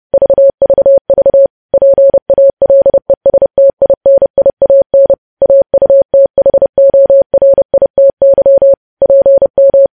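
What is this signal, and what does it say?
Morse code, heard as a keyed tone, about 30 wpm, 550 Hz. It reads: VVV PALESTINIAN AUTHORITY PM
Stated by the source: News headlines sent in Morse code at 30.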